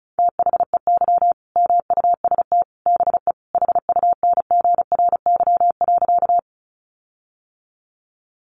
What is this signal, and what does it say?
Morse code, heard as a keyed tone, about 34 wpm, 700 Hz. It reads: THEY MUST BE HUNGRY.
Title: Sentences from Top 1400 Words 35wpm